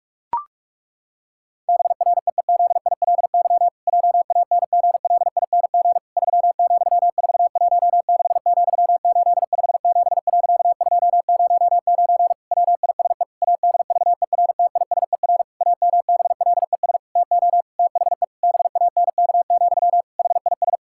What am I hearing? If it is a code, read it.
Morse code, heard as a keyed tone, about 46 wpm, 700 Hz. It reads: BREEZILY JANGLING 3,416,857.209 WISE ADVERTISER AMBLES TO THE BANK, HIS
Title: Pangram Challenge v2 - 45wpm